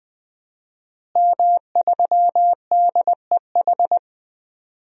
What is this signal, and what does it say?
Morse code, heard as a keyed tone, about 20 wpm, 700 Hz. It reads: M3DEH
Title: DXCC-100 Call Signs - 5 Characters - Repeated 2x - 20wpm